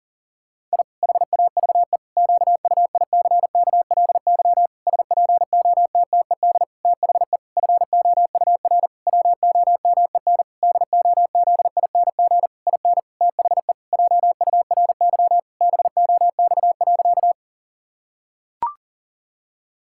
Morse code, heard as a keyed tone, about 40 wpm, 700 Hz. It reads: I HAVE QUICKLY SPOTTED THE FOUR WOMEN DOZING IN THE JURY BOX.
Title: Pangram Challenge v2 - 40wpm